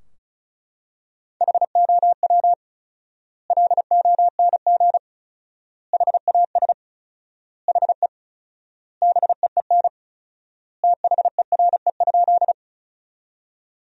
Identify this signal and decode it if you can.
Morse code, heard as a keyed tone, about 36 wpm, 700 Hz. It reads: HOW LONG HAS HE BEEN THERE?
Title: 4x Word-Spacing - Sentences from Top 100 Words - 35wpm